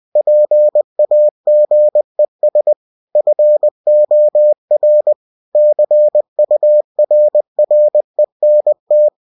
Morse code, heard as a keyed tone, about 20 wpm, 600 Hz. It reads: PAGES FOR CURRENT
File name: cw-books / winnie-the-pooh / winnie-20wpm-page-0122